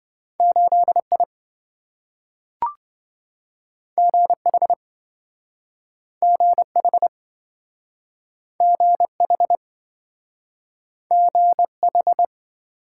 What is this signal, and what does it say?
Morse code, heard as a keyed tone, about 26 wpm, 700 Hz. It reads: 8I GH GH GH GH